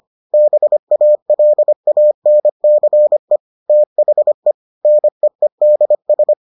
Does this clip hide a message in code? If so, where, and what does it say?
Yes: Morse code from 0.3 to 6.3 s, heard as a keyed tone, about 25 wpm, 600 Hz. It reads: BALANCE THE NEEDS